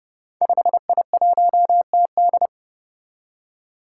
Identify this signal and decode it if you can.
Morse code, heard as a keyed tone, about 28 wpm, 700 Hz. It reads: 5I1TD
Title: Five Challenging Letter-Number Combinations 30wpm